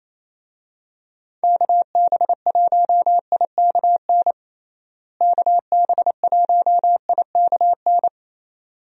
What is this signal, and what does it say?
Morse code, heard as a keyed tone, about 28 wpm, 700 Hz. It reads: KB1IKN KB1IKN
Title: USA Call Signs - 6 Characters - Repeated 2x - 28wpm